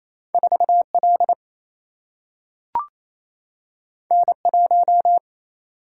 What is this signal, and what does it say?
Morse code, heard as a keyed tone, about 28 wpm, 700 Hz. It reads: 4L N1
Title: Single-Letter Single-Number 28wpm